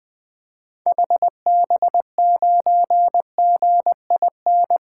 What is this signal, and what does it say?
Morse code, heard as a keyed tone, about 20 wpm, 700 Hz. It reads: HB9GIN
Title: 2023 SOTA Activator Call Signs - 20wpm